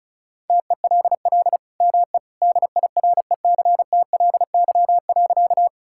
Morse code, heard as a keyed tone, about 36 wpm, 690 Hz. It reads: TELL ME DIRECTLY.
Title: Sentences from Top 1000 Words - Review Part 5 - 35wpm